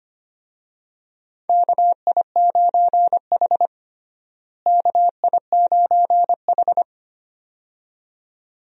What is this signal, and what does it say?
Morse code, heard as a keyed tone, about 25 wpm, 700 Hz. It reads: KI9H KI9H
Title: USA Call Signs - 4 Characters - Repeated 2x - 25wpm